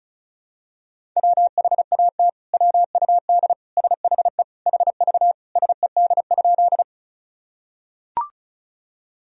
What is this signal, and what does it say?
Morse code, heard as a keyed tone, about 36 wpm, 700 Hz. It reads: WHAT WUD SHE HV SED?
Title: Sentences from Top 100 Words and Abbreviations 35wpm